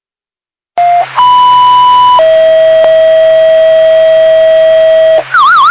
Two-tone page, heard as a one-second tone, then a three-second tone, about 990 then 640 Hz. It is heard from 1.2 to 5.2 s.